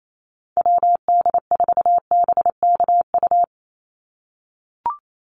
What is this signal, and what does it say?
Morse code, heard as a keyed tone, about 28 wpm, 700 Hz. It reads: WD4BKU
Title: USA Call Signs 28wpm